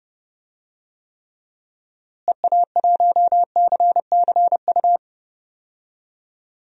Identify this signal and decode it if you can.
Morse code, heard as a keyed tone, about 30 wpm, 700 Hz. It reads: EA1CCU
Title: DXCC-100 Call Signs - 6 Characters - Repeated 3x - 30wpm